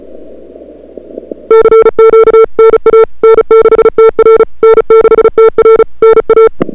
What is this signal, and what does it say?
Morse code, heard as a keyed tone, about 34 wpm, 440 Hz. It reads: CQ NA N6TR N6TR NA
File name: CWSprint / February 2007 CW Sprint / 01-2007 CW Sprint Hour1